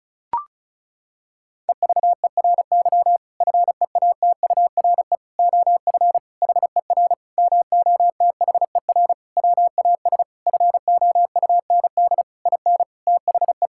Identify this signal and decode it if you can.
Morse code, heard as a keyed tone, about 34 wpm, 700 Hz. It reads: EVERY FEATURE OF HER MOTHER WAS FOUND IN THE
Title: Sentences from Top 1000 Words - Review Part 5 - 35wpm